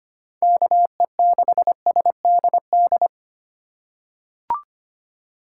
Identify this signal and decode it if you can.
Morse code, heard as a keyed tone, about 25 wpm, 700 Hz. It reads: KE6SDD
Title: USA Call Signs - 6 Characters - 2x Character Spacing - 25wpm